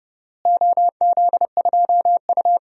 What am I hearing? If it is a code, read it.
Morse code, heard as a keyed tone, about 30 wpm, 700 Hz. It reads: OZ2U
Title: DXCC-100 Call Signs - 4 Characters - Repeated 2x - 30wpm